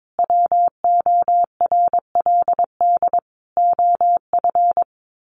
Morse code, heard as a keyed tone, about 22 wpm, 700 Hz. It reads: WORLD OF
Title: A Sound Of Thunder 22wpm